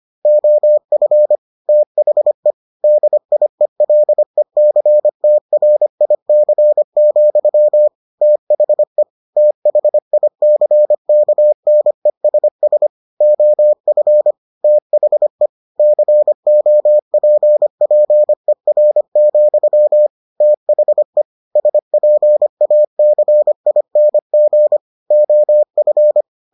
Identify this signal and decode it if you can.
Morse code, heard as a keyed tone, about 25 wpm, 600 Hz. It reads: OF THE DIELECTRIC, THE THICKNESS OF THE COPPER, THE SPACING OF